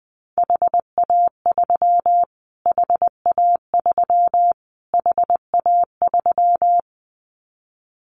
Morse code, heard as a keyed tone, about 20 wpm, 700 Hz. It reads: HA3 HA3 HA3
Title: Callsign Prefixes for Top 25 DXCC - Repeated 3x - 20wpm